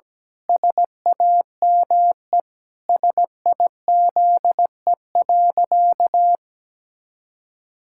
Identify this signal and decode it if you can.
Morse code, heard as a keyed tone, about 17 wpm, 700 Hz. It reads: SAME SIZE.
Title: Pangram Challenge v2 - 17wpm